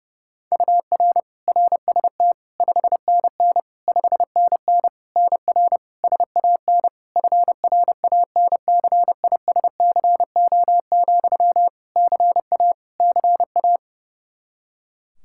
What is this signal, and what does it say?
Morse code, heard as a keyed tone, about 32 wpm, 700 Hz. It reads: UR RST 5NN 5NN NR SAN FRANCISCO, CA CA
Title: Practice QSO - Exchange 1 - Beginning - 30wpm